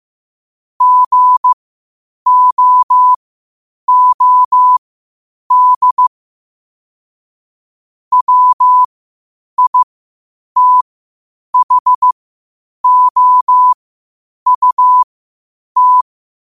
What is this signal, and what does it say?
Morse code, heard as a keyed tone, about 15 wpm, spaced as for 8 wpm, 1000 Hz. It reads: GOOD WITHOUT
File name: qotdfiles / 8wpm